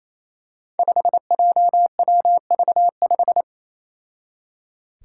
Morse code, heard as a keyed tone, about 28 wpm, 700 Hz. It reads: 5JWV5
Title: Five Challenging Letter-Number Combinations 28wpm